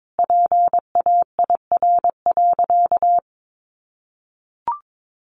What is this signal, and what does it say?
Morse code, heard as a keyed tone, about 22 wpm, 700 Hz. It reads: PAIR.
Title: Sentences from Top 1200 Words 22wpm